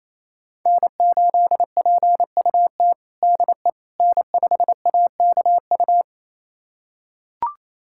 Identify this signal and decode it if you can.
Morse code, heard as a keyed tone, about 28 wpm, 700 Hz. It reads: N8PUT DE N5AKU
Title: USA Callsign DE Callsign - 5 Characters - 28wpm